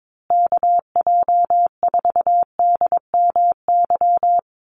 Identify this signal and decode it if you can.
Morse code, heard as a keyed tone, about 22 wpm, 700 Hz. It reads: KJ4DMY
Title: USA Call Signs Repeated 2x 22wpm